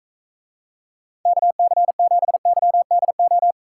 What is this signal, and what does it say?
Morse code, heard as a keyed tone, about 45 wpm, 700 Hz. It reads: KC7YDO